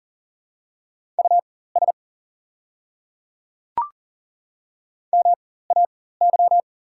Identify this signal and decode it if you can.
Morse code, heard as a keyed tone, about 38 wpm, spaced as for 18 wpm, 690 Hz. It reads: US MAY
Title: 4x Character Spacing - Top 200 Words - 40wpm